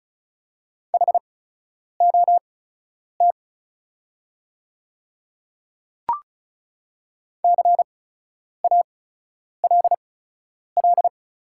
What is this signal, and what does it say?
Morse code, heard as a keyed tone, about 34 wpm, spaced as for 10 wpm, 690 Hz. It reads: HOT CALL
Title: 8x Character Spacing - Top 200 Words - 35wpm